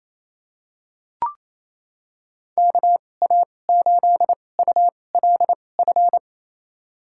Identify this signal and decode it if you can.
Morse code, heard as a keyed tone, about 28 wpm, spaced as for 20 wpm, 700 Hz. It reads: KA8ULF
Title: USA Call Signs - 6 Characters - 2x Character Spacing - 28wpm